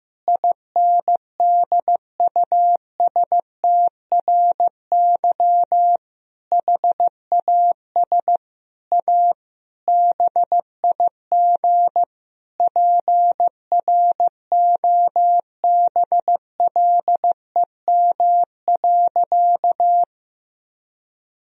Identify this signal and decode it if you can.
Morse code, heard as a keyed tone, about 16 wpm, 700 Hz. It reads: INDUSTRY HAS A BIG PROBLEM.
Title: Sentences from Top 1000 Words - Review Part 4 - 15wpm